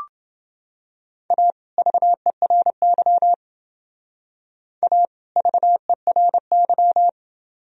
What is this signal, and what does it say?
Morse code, heard as a keyed tone, about 29 wpm, 700 Hz. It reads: A VERY A VERY